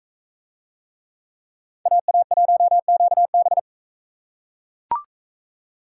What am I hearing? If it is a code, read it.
Morse code, heard as a keyed tone, about 42 wpm, 700 Hz. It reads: AA1QB